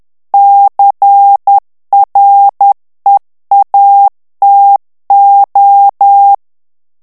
Morse code, heard as a keyed tone, about 10 wpm, 800 Hz. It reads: CREATO
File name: morse mp3 / ITALIEN texte clair 1 vitesse 10 mots minute